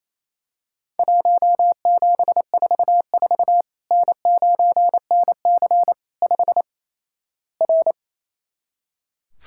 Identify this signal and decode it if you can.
Morse code, heard as a keyed tone, about 28 wpm, 700 Hz. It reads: 1744 N9NC 5 R
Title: Worked All Europe DX Contest - QTC Exchange - 28wpm